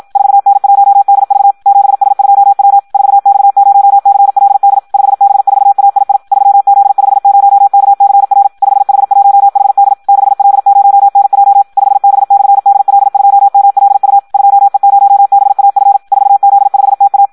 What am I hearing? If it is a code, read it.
Morse code, heard as a keyed tone, about 56 wpm, 800 Hz. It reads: 4N1DV 7S2W 4X0CLD 5B4AII 3Z50KCR 5H1HD 6L0NJ 5B/DL2MLU 2E0BAF 4Z5TA
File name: Audio practica CW / qrq-tx-54 2